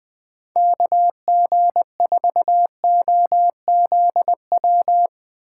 Morse code, heard as a keyed tone, about 20 wpm, 700 Hz. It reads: KG4OZW
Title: USA Call Signs - 6x Character Spacing - 20wpm